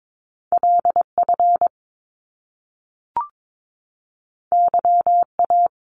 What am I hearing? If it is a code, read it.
Morse code, heard as a keyed tone, about 22 wpm, 700 Hz. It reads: LF YA
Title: Top Two Letter Combinations in Words 22wpm